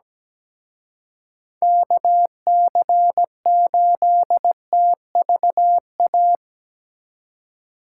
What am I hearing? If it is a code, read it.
Morse code, heard as a keyed tone, about 19 wpm, 700 Hz. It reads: KC8TVA
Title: Speed Racing - USA Call Signs 17wpm